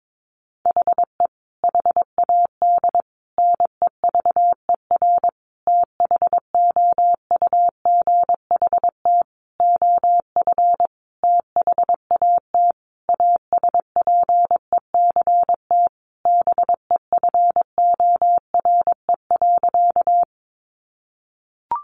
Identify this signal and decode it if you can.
Morse code, heard as a keyed tone, about 22 wpm, 700 Hz. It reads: HE HAD NEVER THOUGHT OF THAT ASPECT BEFORE.
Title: Sentences from Top 1300 Words 22wpm